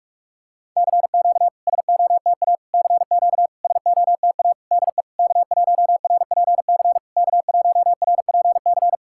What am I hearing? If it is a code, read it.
Morse code, heard as a keyed tone, about 46 wpm, 690 Hz. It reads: CQ SOTA CQ SOTA DE K1RPC K1RPC
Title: SOTA QSO Practice 45wpm